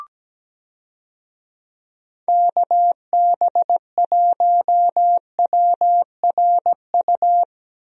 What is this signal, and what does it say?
Morse code, heard as a keyed tone, about 17 wpm, 700 Hz. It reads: KB1WRU
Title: USA Call Signs - 6 Characters - 17wpm